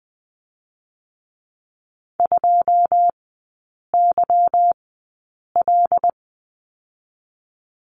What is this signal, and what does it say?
Morse code, heard as a keyed tone, about 20 wpm, 700 Hz. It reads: 2 Y L
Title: 145 Lesson 24e - Letter y - Three Characters 20wpm